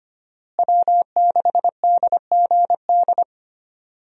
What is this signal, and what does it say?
Morse code, heard as a keyed tone, about 25 wpm, 700 Hz. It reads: W6DGD
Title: USA Call Signs - 5 Characters - Repeated 3x - 25wpm